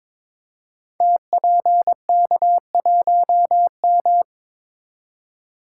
Morse code, heard as a keyed tone, about 23 wpm, 700 Hz. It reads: TPK1M
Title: Code Groups - 5 Letter:Numbers - 22wpm